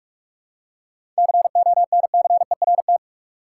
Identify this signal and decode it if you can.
Morse code, heard as a keyed tone, about 46 wpm, 700 Hz. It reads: CONCERT
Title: Seven Letter Words - 45wpm